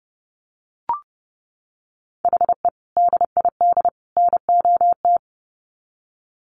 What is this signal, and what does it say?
Morse code, heard as a keyed tone, about 30 wpm, 700 Hz. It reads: HE DID NOT
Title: Sets of 3 Words 30wpm